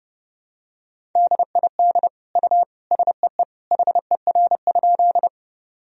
Morse code, heard as a keyed tone, about 30 wpm, 700 Hz. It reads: DID U SEE HER?
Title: Sentences from Top 100 Words and Abbreviations 30wpm